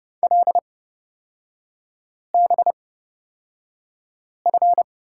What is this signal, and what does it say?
Morse code, heard as a keyed tone, about 30 wpm, spaced as for 5 wpm, 700 Hz. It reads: LBF
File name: course / 30:10wpm / 236 Lesson 38e - Letter q - Three Characters 30wpm